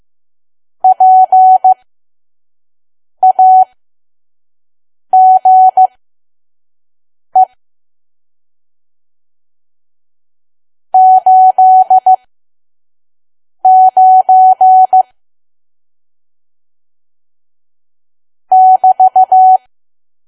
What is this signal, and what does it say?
Morse code, heard as a keyed tone, about 15 wpm, spaced as for 5 wpm, 750 Hz. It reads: PAGE 89 <BT>
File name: Archive / 5 WPM files / PM